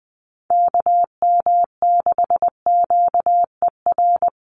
Morse code, heard as a keyed tone, about 20 wpm, 700 Hz. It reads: KM6QER
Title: USA Call Signs Repeated 3x 20wpm